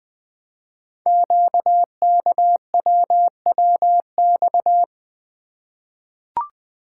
Morse code, heard as a keyed tone, about 20 wpm, 700 Hz. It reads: QKWWX
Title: Code Groups - 5 Letters - 20wpm